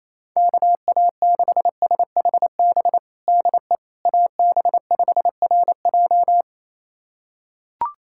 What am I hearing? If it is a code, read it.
Morse code, heard as a keyed tone, about 28 wpm, 700 Hz. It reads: KA6SHB DE AB5RJ
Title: USA Callsign DE Callsign 28wpm